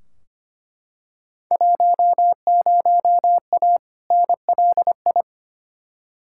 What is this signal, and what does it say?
Morse code, heard as a keyed tone, about 25 wpm, 700 Hz. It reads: 10A NLI
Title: Field Day Exchanges Encore - 25wpm